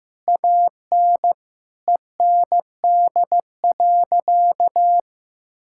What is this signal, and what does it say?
Morse code, heard as a keyed tone, about 15 wpm, 700 Hz. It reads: AN END.
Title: Sentences from Top 1000 Words - Review Part 5 - 15wpm